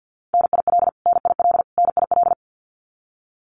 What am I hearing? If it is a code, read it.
Morse code, heard as a keyed tone, about 50 wpm, 700 Hz. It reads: NIL NIL NIL
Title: Abbreviations Repeated 3x - 50wpm